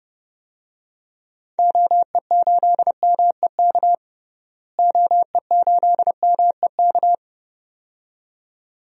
Morse code, heard as a keyed tone, about 31 wpm, 700 Hz. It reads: OE8MEK OE8MEK
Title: DXCC-100 Call Signs - 6 Characters - Repeated 2x - 30wpm